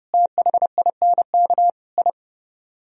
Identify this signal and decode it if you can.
Morse code, heard as a keyed tone, about 37 wpm, 700 Hz. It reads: THINK I
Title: Speed-Racing - Sets of 5 Words 30wpm